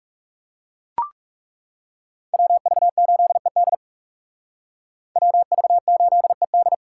Morse code, heard as a keyed tone, about 36 wpm, 700 Hz. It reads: WV8ED WV8ED
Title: Speed Racing - USA 2x2 Call Signs 30wpm